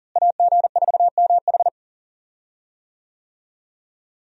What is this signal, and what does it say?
Morse code, heard as a keyed tone, about 38 wpm, 700 Hz. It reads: AG4MH
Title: USA Call Signs - 5 Characters - Repeated 3x - 40wpm